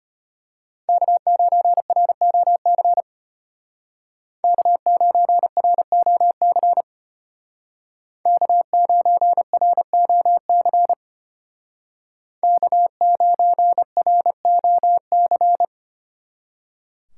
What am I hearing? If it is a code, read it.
Morse code, heard as a keyed tone, about 33 wpm, 700 Hz. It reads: K9ROC K9ROC K9ROC K9ROC